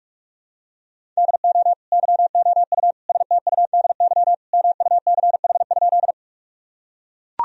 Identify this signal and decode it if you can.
Morse code, heard as a keyed tone, about 41 wpm, 700 Hz. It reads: DO YOU STUDY MUCH?
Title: Sentences from Top 1000 Words - Review Part 1 - 45wpm